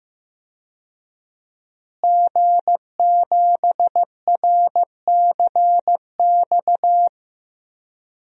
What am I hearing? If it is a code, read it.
Morse code, heard as a keyed tone, about 15 wpm, 700 Hz. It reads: G7RCX
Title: DXCC-100 Call Signs - 5 Characters - Repeated 3x - 15wpm